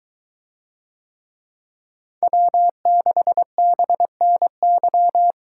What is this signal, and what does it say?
Morse code, heard as a keyed tone, about 22 wpm, 700 Hz. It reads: W6BNY